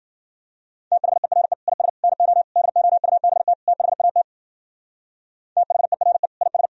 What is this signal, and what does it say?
Morse code, heard as a keyed tone, about 50 wpm, 700 Hz. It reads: THERE IS NO DOUBT THAT THERE IS